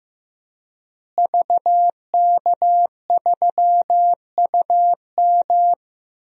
Morse code, heard as a keyed tone, about 15 wpm, 700 Hz. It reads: VK3UM
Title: DXCC-100 Call Signs - 5 Characters - Repeated 3x - 15wpm